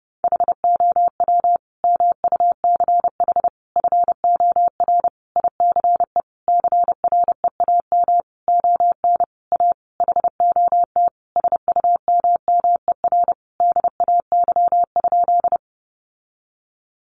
Morse code, heard as a keyed tone, about 30 wpm, 700 Hz. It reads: HOW MUCH FOR ICE CREAM ON A HOT SUMMER DAY?
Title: Sentences from Top 1500 Words 30wpm